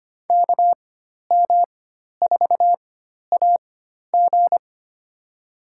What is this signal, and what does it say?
Morse code, heard as a keyed tone, about 25 wpm, spaced as for 12 wpm, 700 Hz. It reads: KM4AG
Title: USA Call Signs - 5 Characters - 4x Character Spacing - 25wpm